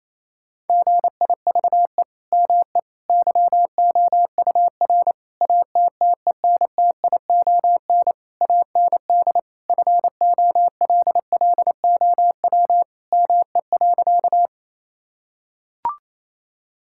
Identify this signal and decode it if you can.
Morse code, heard as a keyed tone, about 28 wpm, 700 Hz. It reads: GIVE ME YOUR ATTENTION AND FOLLOW ME.
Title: Sentences from Top 1000 Words - Review Part 6 - 28wpm